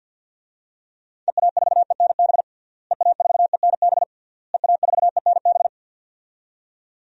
Morse code, heard as a keyed tone, about 52 wpm, 700 Hz. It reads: EA4ENB EA4ENB EA4ENB
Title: DXCC-100 Call Signs - 6 Characters - Repeated 3x - 50wpm